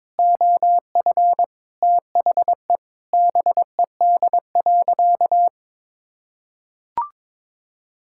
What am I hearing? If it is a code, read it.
Morse code, heard as a keyed tone, about 23 wpm, 700 Hz. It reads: OF THE BED.
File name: practice / 22wpm / Sentences with 3-Word Phrases - 22wpm